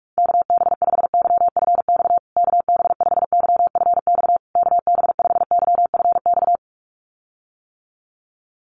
Morse code, heard as a keyed tone, about 44 wpm, 690 Hz. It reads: KB5YFX KB5YFX KB5YFX
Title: USA Call Signs Repeated 3x 45wpm